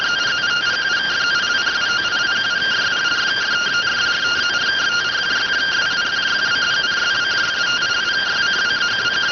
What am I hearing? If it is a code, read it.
Two-tone teleprinter signal, about 170 Hz shift.